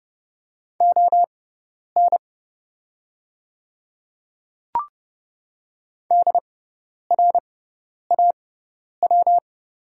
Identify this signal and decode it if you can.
Morse code, heard as a keyed tone, about 30 wpm, spaced as for 10 wpm, 690 Hz. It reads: ON DRAW
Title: 6x Character Spacing - Top 500 Words - 30wpm